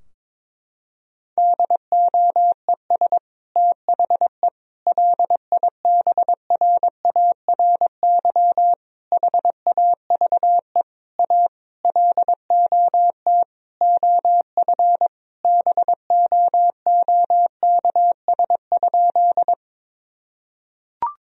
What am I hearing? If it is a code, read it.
Morse code, heard as a keyed tone, about 22 wpm, 700 Hz. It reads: DOES THE LIBRARY HAVE A LOT OF BOOKS?
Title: Sentences from Top 1000 Words - Review Part 4 - 22wpm